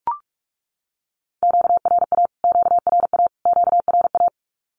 Morse code, heard as a keyed tone, about 44 wpm, 700 Hz. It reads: QRA QRA QRA
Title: Q-Codes Repeated 3x 45wpm